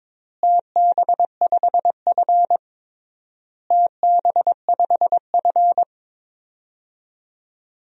Morse code, heard as a keyed tone, about 22 wpm, 700 Hz. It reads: TB5F TB5F
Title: DXCC-100 Call Signs - 4 Characters - Repeated 3x - 22wpm